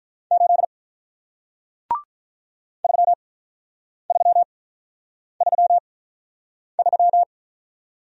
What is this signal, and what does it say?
Morse code, heard as a keyed tone, about 46 wpm, spaced as for 9 wpm, 700 Hz. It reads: 8 3333